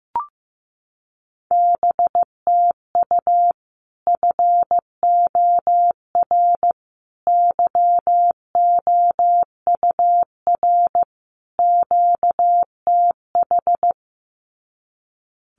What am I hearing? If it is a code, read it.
Morse code, heard as a keyed tone, about 15 wpm, 700 Hz. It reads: BTU FOR YOUR QTH
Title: Common QSO Phrases 15wpm